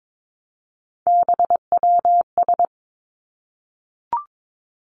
Morse code, heard as a keyed tone, about 22 wpm, 700 Hz. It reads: BWS
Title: Three Challenging Letter-Number Combinations 22wpm